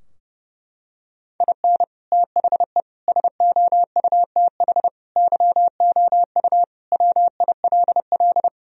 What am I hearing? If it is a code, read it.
Morse code, heard as a keyed tone, about 30 wpm, 700 Hz. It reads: IN THE SOUTH YOU WILL
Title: Sentences from Top 1000 Words - Review Part 2 - 30wpm